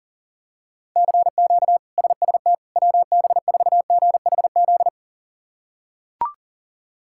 Morse code, heard as a keyed tone, about 37 wpm, 700 Hz. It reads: CQ SST WB4GHZ
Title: K1USN SST QSO Practice - 40wpm